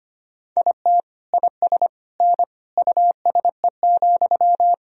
Morse code, heard as a keyed tone, about 25 wpm, 700 Hz. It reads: IT IS N USE,
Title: Sentences from Top 500 Words and Abbreviations 25wpm